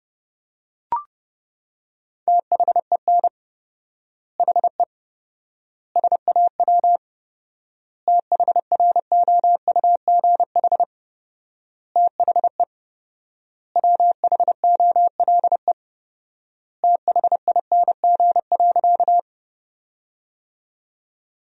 Morse code, heard as a keyed tone, about 32 wpm, 700 Hz. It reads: THEN HE SAW THROUGH THE WHOLE THING.
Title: 4x Word-Spacing - Sentences from Top 400 Words - 30wpm